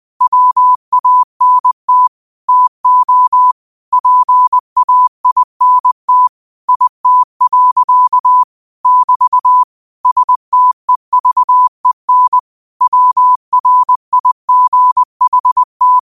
Morse code, heard as a keyed tone, about 20 wpm, 1000 Hz. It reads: WANT TO PAINT IT. <BT> STEVEN WRIGHT